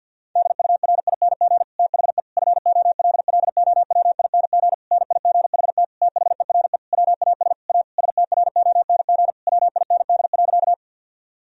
Morse code, heard as a keyed tone, about 50 wpm, 700 Hz. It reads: DURING THE FOLLOWING NIGHT THERE WAS A STRONG WIND.
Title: Sentences from Top 1000 Words - Review Part 4 - 50wpm